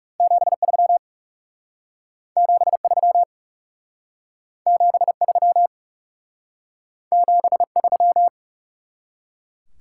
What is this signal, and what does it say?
Morse code, heard as a keyed tone, about 39 wpm, 700 Hz. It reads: 73 73 73 73